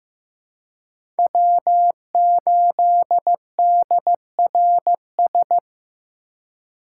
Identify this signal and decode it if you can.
Morse code, heard as a keyed tone, about 15 wpm, 700 Hz. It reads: W8DRS
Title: USA Call Signs - 5 Characters - 3x Character Spacing - 15wpm